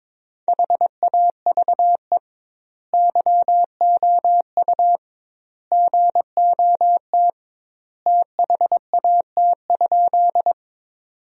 Morse code, heard as a keyed tone, about 22 wpm, 700 Hz. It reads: HAVE YOU GOT THAT?
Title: 2x Word-Spacing - Sentences from Top 300 Words - 22wpm